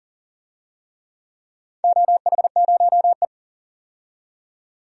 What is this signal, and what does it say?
Morse code, heard as a keyed tone, about 40 wpm, 700 Hz. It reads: OH0E
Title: DXCC-100 Call Signs - 4 Characters - 40wpm